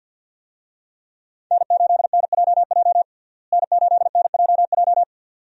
Morse code, heard as a keyed tone, about 51 wpm, 700 Hz. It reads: N8NJJ N8NJJ
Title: USA Call Signs - 5 Characters - Repeated 2x - 50wpm